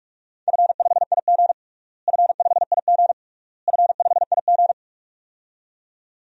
Morse code, heard as a keyed tone, about 47 wpm, 700 Hz. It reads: F5IG F5IG F5IG
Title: DXCC-100 Call Signs - 4 Characters - Repeated 3x - 45wpm